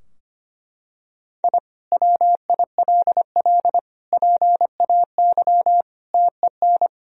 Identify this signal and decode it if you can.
Morse code, heard as a keyed tone, about 25 wpm, 700 Hz. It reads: I WILL PAY TEN
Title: Sentences from Top 1000 Words - Review Part 2 - 25wpm